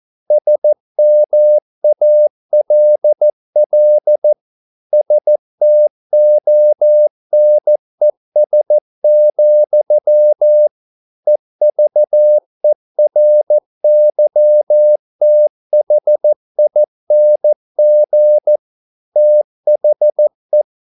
Morse code, heard as a keyed tone, about 14 wpm, 600 Hz. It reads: SMALL STONES, EVERYTHING THE